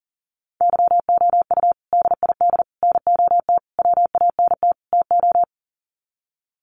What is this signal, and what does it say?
Morse code, heard as a keyed tone, about 40 wpm, 700 Hz. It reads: YOU DID NOT WANT TO
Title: Sets of 5 Words Encore2 40wpm